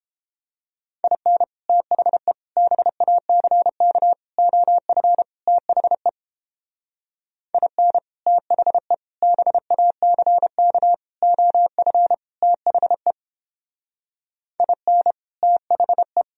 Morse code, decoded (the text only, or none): IN THE BACK OF THE IN THE BACK OF THE IN THE